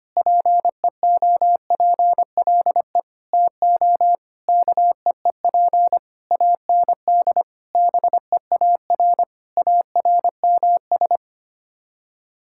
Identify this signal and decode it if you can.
Morse code, heard as a keyed tone, about 25 wpm, 700 Hz. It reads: PEOPLE TO KEEP AND BEAR ARMS
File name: Sets of 9 Words - 25wpm